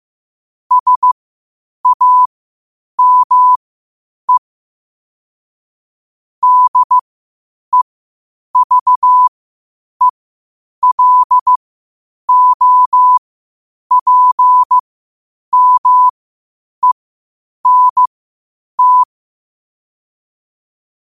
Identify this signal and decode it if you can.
Morse code, heard as a keyed tone, about 15 wpm, spaced as for 8 wpm, 1000 Hz. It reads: SAME DEVELOPMENT
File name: qotdfiles / 8wpm